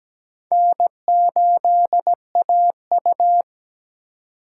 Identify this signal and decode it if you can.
Morse code, heard as a keyed tone, about 20 wpm, 700 Hz. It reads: N8AU
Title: Speed Racing - USA 1x2 Call Signs 17wpm